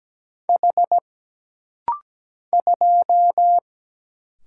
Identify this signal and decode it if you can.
Morse code, heard as a keyed tone, about 17 wpm, 700 Hz. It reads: H 2
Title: Single Letter-Number - Rapid-Fire 17wpm